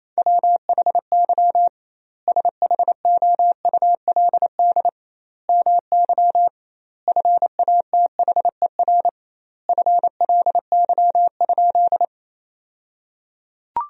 Morse code, heard as a keyed tone, about 28 wpm, 700 Hz. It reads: WHY SHOULD MY FATHER FLY?
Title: 2x Word-Spacing - Sentences from Top 500 Words - 28wpm